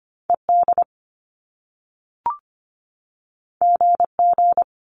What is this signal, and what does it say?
Morse code, heard as a keyed tone, about 25 wpm, 700 Hz. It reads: E D G G
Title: Top Two Letter Combinations in Words 25wpm